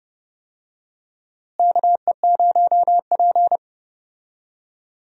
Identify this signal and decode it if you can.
Morse code, heard as a keyed tone, about 30 wpm, 700 Hz. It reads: KE0P
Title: USA Call Signs - 4 Characters - Repeated 2x - 30wpm